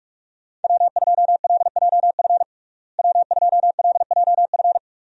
Morse code, heard as a keyed tone, about 46 wpm, 700 Hz. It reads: W2LJF W2LJF
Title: USA Call Signs - 5 Characters - Repeated 2x - 45wpm